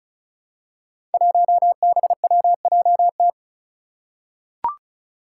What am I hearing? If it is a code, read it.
Morse code, heard as a keyed tone, about 36 wpm, 700 Hz. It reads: 1BWJT
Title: Five Challenging Letter-Number Combinations 35wpm